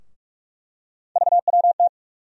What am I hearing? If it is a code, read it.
Morse code, heard as a keyed tone, about 47 wpm, 700 Hz. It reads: VWT
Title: USA Call Sign Postfixes - 3 Letters 45wpm